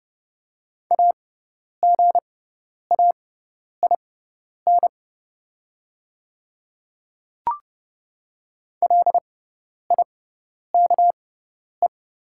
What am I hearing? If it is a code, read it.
Morse code, heard as a keyed tone, about 30 wpm, spaced as for 10 wpm, 700 Hz. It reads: AGAIN LIKE
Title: 6x Character Spacing - Top 200 Words - 30wpm